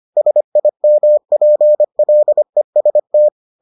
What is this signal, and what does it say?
Morse code, heard as a keyed tone, about 25 wpm, 600 Hz. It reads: SIMPLEST